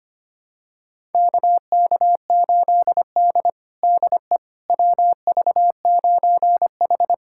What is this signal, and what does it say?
Morse code, heard as a keyed tone, about 25 wpm, 700 Hz. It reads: KK8D DE WV9H
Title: USA Callsign DE Callsign - 4 Characters - 25wpm